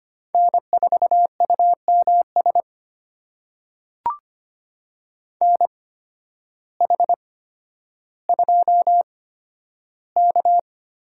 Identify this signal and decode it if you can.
Morse code, heard as a keyed tone, about 25 wpm, spaced as for 12 wpm, 700 Hz. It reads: N4UMS NH2K
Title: USA Call Signs - 5 Characters - 8x Character Spacing - 25wpm